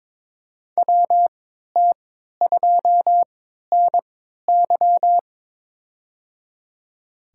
Morse code, heard as a keyed tone, about 22 wpm, spaced as for 12 wpm, 700 Hz. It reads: WT2NY
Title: USA Call Signs - 3x Character Spacing - 22wpm